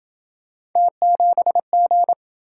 Morse code, heard as a keyed tone, about 27 wpm, 700 Hz. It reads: T7G